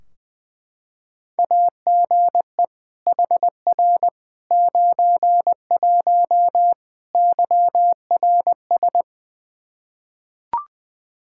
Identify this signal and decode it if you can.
Morse code, heard as a keyed tone, about 21 wpm, 700 Hz. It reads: AGE HR 91 YRS
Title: Practice QSO - Exchange 3 - More Info - 20wpm